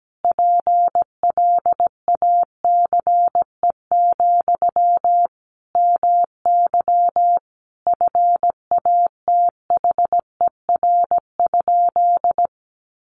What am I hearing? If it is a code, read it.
Morse code, heard as a keyed tone, about 17 wpm, 700 Hz. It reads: PLACE, MY FATHER?
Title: Sentences from Top 400 Words 17wpm